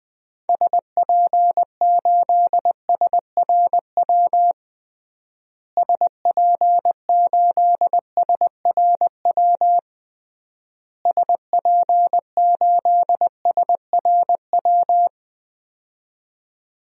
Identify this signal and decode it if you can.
Morse code, heard as a keyed tone, about 20 wpm, 700 Hz. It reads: SP8SRW SP8SRW SP8SRW
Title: DXCC-100 Call Signs - 6 Characters - Repeated 3x - 20wpm